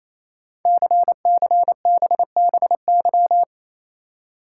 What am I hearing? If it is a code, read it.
Morse code, heard as a keyed tone, about 27 wpm, 700 Hz. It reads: CCBBY
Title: Code Groups - 5 Letters - 28wpm